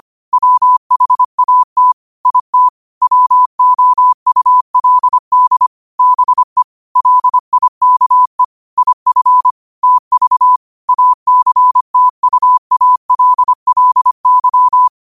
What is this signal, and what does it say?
Morse code, heard as a keyed tone, about 25 wpm, 1000 Hz. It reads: WHAT IT WOULD BE LIKE IF TV ACTUALLY
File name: qotdfiles / 25wpm